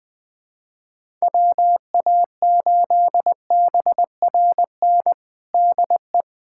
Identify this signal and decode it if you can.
Morse code, heard as a keyed tone, about 20 wpm, 700 Hz. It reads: WA8BRN DE
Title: USA Callsign DE Callsign - 6 Characters - 20wpm